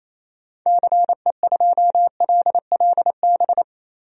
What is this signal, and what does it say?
Morse code, heard as a keyed tone, about 28 wpm, 700 Hz. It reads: CE2LLB
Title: DXCC-100 Call Signs - 6 Characters - 28wpm